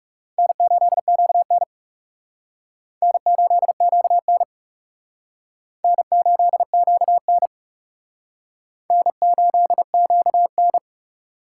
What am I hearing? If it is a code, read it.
Morse code, heard as a keyed tone, about 38 wpm, 690 Hz. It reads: N8QN N8QN N8QN N8QN